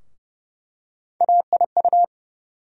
Morse code, heard as a keyed tone, about 29 wpm, 700 Hz. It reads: AIU
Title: USA Call Sign Postfixes - 3 Letters 30wpm